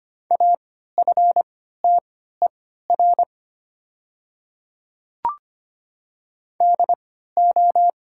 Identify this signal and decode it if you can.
Morse code, heard as a keyed tone, about 25 wpm, spaced as for 14 wpm, 700 Hz. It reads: AFTER DO
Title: 3x Character Spacing - Top 500 Words - 25wpm